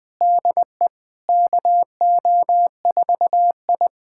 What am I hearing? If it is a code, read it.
Morse code, heard as a keyed tone, about 21 wpm, 700 Hz. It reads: DE KO4I
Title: USA Callsign DE Callsign - 4 Characters - 20wpm